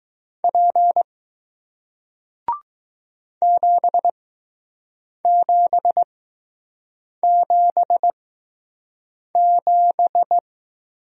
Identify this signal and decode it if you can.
Morse code, heard as a keyed tone, about 20 wpm, spaced as for 6 wpm, 700 Hz. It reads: P 7777